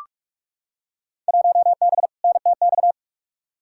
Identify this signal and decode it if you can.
Morse code, heard as a keyed tone, about 41 wpm, 700 Hz. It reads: 1B NTX
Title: Field Day Exchanges - 45wpm